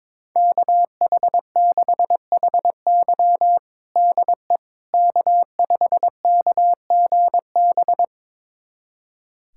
Morse code, heard as a keyed tone, about 22 wpm, 700 Hz. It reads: KH6HY DE K5KGB
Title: USA Callsign DE Callsign - 5 Characters - 22wpm